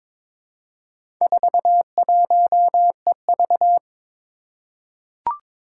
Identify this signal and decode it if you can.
Morse code, heard as a keyed tone, about 22 wpm, 700 Hz. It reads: 41EV
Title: Four Challenging Letter-Number Combinations 22wpm